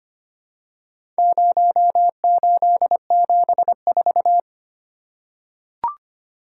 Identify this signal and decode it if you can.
Morse code, heard as a keyed tone, about 25 wpm, 700 Hz. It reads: 0874
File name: practice / 25wpm / Four Letter-Number Combinations - 25wpm